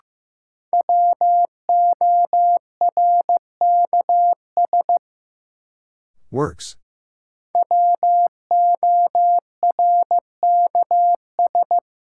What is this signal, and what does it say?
Morse code, heard as a keyed tone, about 15 wpm, 700 Hz. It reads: WORKS WORKS
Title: Sentences from Top 1000 Words - Review Part 5 - New Word Drill - 15wpm